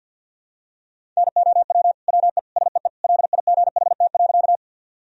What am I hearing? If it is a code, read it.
Morse code, heard as a keyed tone, about 53 wpm, 700 Hz. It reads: NOW WE SEE LIGHT.
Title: Sentences from Top 1000 Words - Review Part 1 - 50wpm